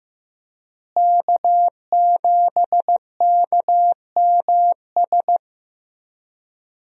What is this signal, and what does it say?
Morse code, heard as a keyed tone, about 15 wpm, 700 Hz. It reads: K7KMS
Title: USA Call Signs - 5 Characters - Repeated 3x - 15wpm